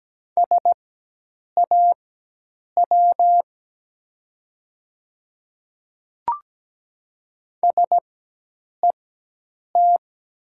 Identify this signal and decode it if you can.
Morse code, heard as a keyed tone, about 17 wpm, spaced as for 8 wpm, 700 Hz. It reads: SAW SET
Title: 4x Character Spacing - Top 300 Words - 17wpm